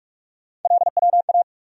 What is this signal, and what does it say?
Morse code, heard as a keyed tone, about 46 wpm, 700 Hz. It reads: LWA
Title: USA Call Sign Postfixes - 3 Letters 45wpm